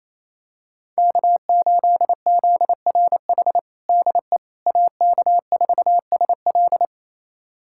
Morse code, heard as a keyed tone, about 28 wpm, 700 Hz. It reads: K8ZRH DE AK4SL
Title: USA Callsign DE Callsign - 5 Characters - 28wpm